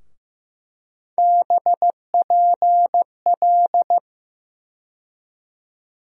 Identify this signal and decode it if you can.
Morse code, heard as a keyed tone, about 15 wpm, 700 Hz. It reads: BPL
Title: USA Call Sign Postfixes - 3 Letters 15wpm